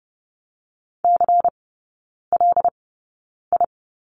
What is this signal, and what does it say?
Morse code, heard as a keyed tone, about 30 wpm, spaced as for 9 wpm, 700 Hz. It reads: CLI
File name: Lesson 18e - Letter m - Three Characters 30wpm